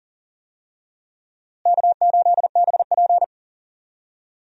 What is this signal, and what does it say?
Morse code, heard as a keyed tone, about 40 wpm, 700 Hz. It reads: K8BP
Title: USA Call Signs - 4 Characters - Repeated 3x - 40wpm